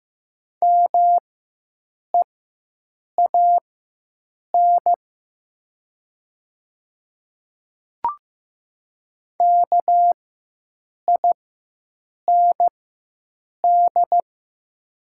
Morse code, heard as a keyed tone, about 15 wpm, spaced as for 7 wpm, 700 Hz. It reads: MEAN KIND
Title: 4x Character Spacing - Top 300 Words - 15wpm